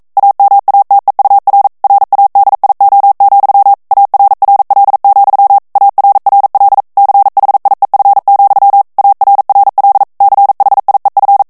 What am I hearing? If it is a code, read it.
Morse code, heard as a keyed tone, about 42 wpm, 800 Hz. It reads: AMATEUR RADIO, ARRL, ARRL CHIEF, ARRL CHIEF
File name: 40 mots minute